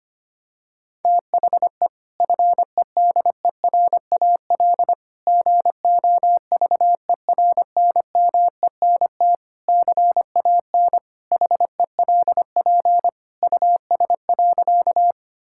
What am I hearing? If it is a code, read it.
Morse code, heard as a keyed tone, about 25 wpm, 700 Hz. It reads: THE FEDERAL GOVERNMENT CAN HELP US.
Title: Sentences from Top 1000 Words - Review Part 5 - 25wpm